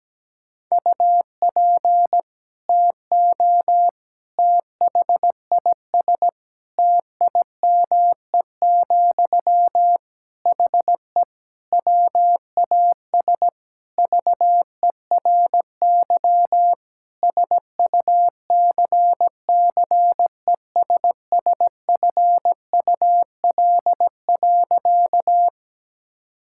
Morse code, heard as a keyed tone, about 17 wpm, 700 Hz. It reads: UP TO THIS TIME, HE WAS VERY SUCCESSFUL.
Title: Sentences from Top 1000 Words - Review Part 6 - 17wpm